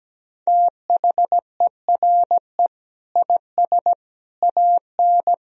Morse code, heard as a keyed tone, about 17 wpm, 700 Hz. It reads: THERE IS AN
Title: Sentences from Top 1000 Words - Review Part 3 - 17wpm